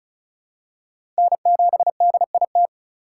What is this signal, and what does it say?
Morse code, heard as a keyed tone, about 36 wpm, 700 Hz. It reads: N7DIT
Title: USA Call Signs - 5 Characters - 2x Character Spacing - 35wpm